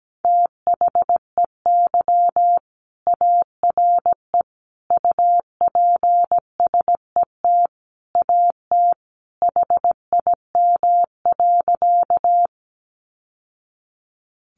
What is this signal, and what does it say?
Morse code, heard as a keyed tone, about 17 wpm, 700 Hz. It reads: THEY ARE UPSET AT HIM.
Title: Sentences from Top 800 Words 17wpm